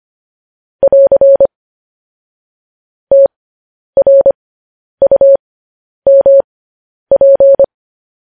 Morse code, heard as a keyed tone, about 25 wpm, spaced as for 10 wpm, 550 Hz. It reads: <AR> TRUMP
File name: podcast / News headlines-1x-25x10